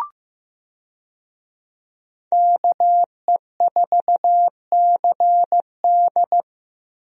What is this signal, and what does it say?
Morse code, heard as a keyed tone, about 15 wpm, 700 Hz. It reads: KE4CD
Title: USA Call Signs - 5 Characters - 15wpm